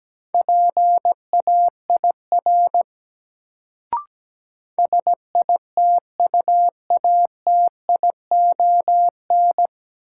Morse code, heard as a keyed tone, about 17 wpm, 700 Hz. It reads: PAIR SITUATION
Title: Sentences from Top 1200 Words - New Word Drill - 17wpm